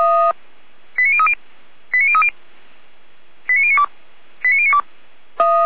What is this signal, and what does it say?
Five-tone selective call (ZVEI), heard as quick runs of five tones, tones of about 70 ms. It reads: 89030 89030 89092 89092
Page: /455/SDR/ZVEI/89510/